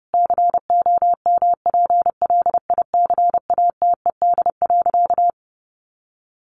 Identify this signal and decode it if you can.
Morse code, heard as a keyed tone, about 30 wpm, 700 Hz. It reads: COMPLICATED.
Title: Sentences from Top 1200 Words 30wpm